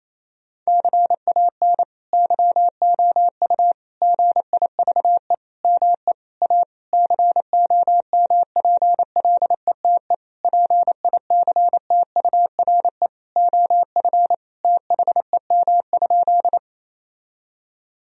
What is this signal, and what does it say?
Morse code, heard as a keyed tone, about 27 wpm, 700 Hz. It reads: CAN YOU GIVE ME A COMPLETE PICTURE OF THEM?
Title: Sentences from Top 1000 Words - Review Part 2 - 28wpm